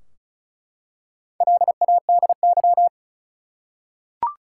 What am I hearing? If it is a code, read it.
Morse code, heard as a keyed tone, about 34 wpm, 700 Hz. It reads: LADY
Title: Four Letter Words - 35wpm